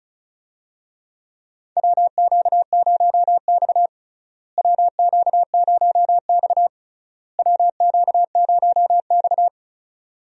Morse code, heard as a keyed tone, about 34 wpm, 700 Hz. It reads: WQ0X WQ0X WQ0X
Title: USA Call Signs - 4 Characters - Repeated 3x - 35wpm